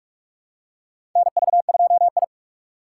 Morse code, heard as a keyed tone, about 44 wpm, 700 Hz. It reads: NV2I
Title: USA Call Signs - 4 Characters - 2x Character Spacing - 45wpm